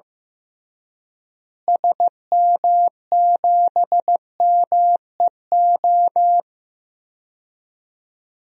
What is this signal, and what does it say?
Morse code, heard as a keyed tone, about 15 wpm, 700 Hz. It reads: SM7MEO
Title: DXCC-100 Call Signs - 6 Characters - Repeated 3x - 15wpm